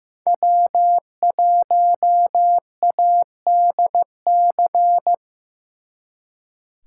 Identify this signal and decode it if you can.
Morse code, heard as a keyed tone, about 18 wpm, 700 Hz. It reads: W1ADC
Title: Speed Racing - USA 1x3 Call Signs 15wpm